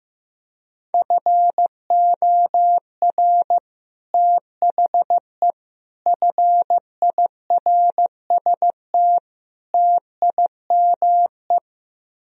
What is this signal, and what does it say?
Morse code, heard as a keyed tone, about 15 wpm, 700 Hz. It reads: FOR THE FIRST TIME
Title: Speed-Racing - Sets of 4 Words 15wpm